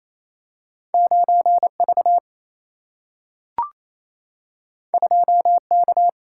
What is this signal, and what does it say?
Morse code, heard as a keyed tone, about 28 wpm, 700 Hz. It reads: 9V 2K
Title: Single-Letter Single-Number 28wpm